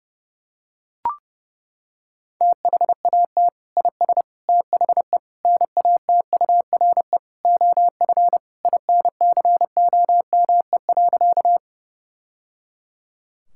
Morse code, heard as a keyed tone, about 30 wpm, 700 Hz. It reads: THAT IS THE NATURE OF INCOME.
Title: Sentences from Top 1000 Words - Review Part 5 - 30wpm